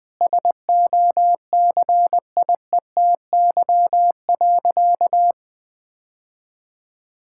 Morse code, heard as a keyed tone, about 20 wpm, 700 Hz. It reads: SOCIETY.
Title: Sentences from Top 1600 Words 20wpm